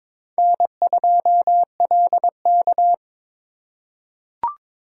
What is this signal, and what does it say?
Morse code, heard as a keyed tone, about 22 wpm, 700 Hz. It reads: N2LK
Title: K1USN SST QSO Practice - 22wpm